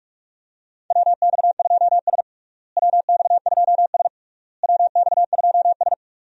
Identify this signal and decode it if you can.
Morse code, heard as a keyed tone, about 46 wpm, 700 Hz. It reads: WX2S WX2S WX2S
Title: USA Call Signs - 4 Characters - Repeated 3x - 45wpm